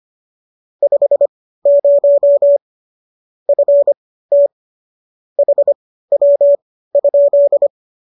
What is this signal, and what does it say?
Morse code, heard as a keyed tone, about 25 wpm, spaced as for 15 wpm, 590 Hz. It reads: 50 FT HW?